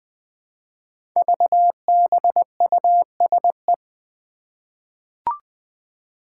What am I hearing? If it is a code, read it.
Morse code, heard as a keyed tone, about 20 wpm, 700 Hz. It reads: VBUSE
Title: Five Challenging Letter-Number Combinations 20wpm